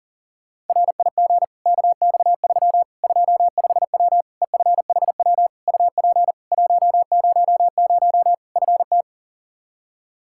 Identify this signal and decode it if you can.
Morse code, heard as a keyed tone, about 37 wpm, 700 Hz. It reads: RIG KX3 25W EFHW UP 100 FT
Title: Practice QSO - Exchange 2 - Info - 40wpm